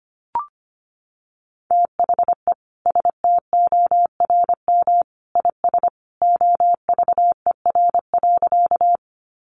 Morse code, heard as a keyed tone, about 25 wpm, 700 Hz. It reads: THE STORM IS OVER.
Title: Sentences from Top 1500 Words 25wpm